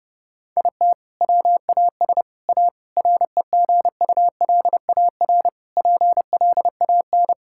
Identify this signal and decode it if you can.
Morse code, heard as a keyed tone, about 30 wpm, 700 Hz. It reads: IT WAS A REGULAR PLAN
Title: Sentences from Top 1000 Words - Review Part 6 - 30wpm